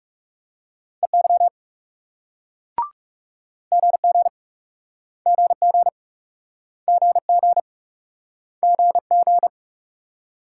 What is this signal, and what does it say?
Morse code, heard as a keyed tone, about 38 wpm, 700 Hz. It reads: EY GG GG GG GG